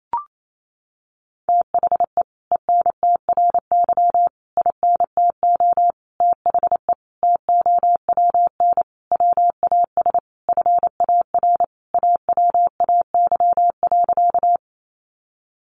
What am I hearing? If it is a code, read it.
Morse code, heard as a keyed tone, about 28 wpm, 700 Hz. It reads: THE ENTRY INTO THE TOWN WAS FAR AWAY.
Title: Sentences from Top 1500 Words 28wpm